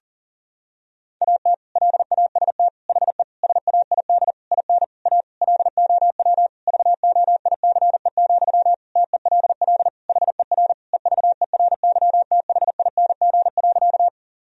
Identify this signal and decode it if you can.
Morse code, heard as a keyed tone, about 40 wpm, 700 Hz. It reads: AT LAST HE SAID IN A LOW VOICE, TELL HER EVERYTHING.
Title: Sentences from Top 1000 Words - Review Part 1 - 40wpm